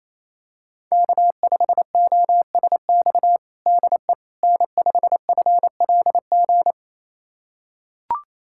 Morse code, heard as a keyed tone, about 28 wpm, 700 Hz. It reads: K5OSX DE N5FLG
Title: USA Callsign DE Callsign - 5 Characters - 28wpm